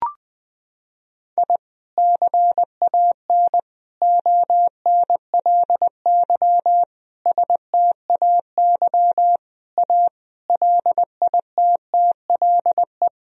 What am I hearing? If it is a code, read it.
Morse code, heard as a keyed tone, about 20 wpm, 700 Hz. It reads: I CAN ONLY STAY A LITTLE
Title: Sentences from Top 1000 Words - Review Part 3 - 20wpm